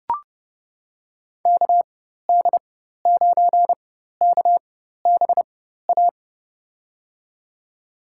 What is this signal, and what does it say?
Morse code, heard as a keyed tone, about 30 wpm, spaced as for 14 wpm, 700 Hz. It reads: KD9KBA
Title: USA Call Signs - 6 Characters - 4x Character Spacing - 30wpm